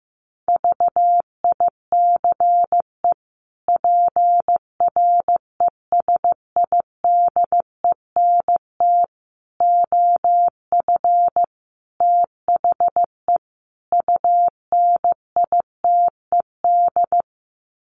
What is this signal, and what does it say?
Morse code, heard as a keyed tone, about 15 wpm, 700 Hz. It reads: VICE PRESIDENT OF THE UNITED
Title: Sets of 5 Words Encore2 15wpm